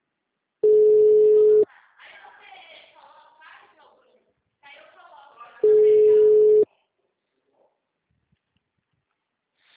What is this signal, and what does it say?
Ringback tone (Europe, 425 Hz), with speech.
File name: out- -2006- - - .292